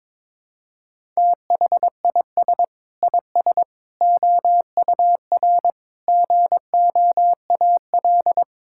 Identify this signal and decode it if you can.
Morse code, heard as a keyed tone, about 22 wpm, 700 Hz. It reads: THIS IS OUR GOAL
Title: Sentences from Top 1000 Words - Review Part 5 - 22wpm